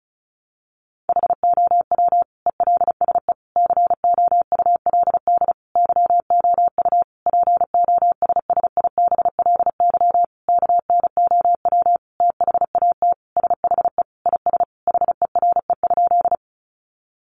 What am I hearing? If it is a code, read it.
Morse code, heard as a keyed tone, about 34 wpm, 700 Hz. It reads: HOW ELSE COULD YOU POSSIBLY KNOW THAT SHE IS HERE?
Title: Sentences from Top 1300 Words 35wpm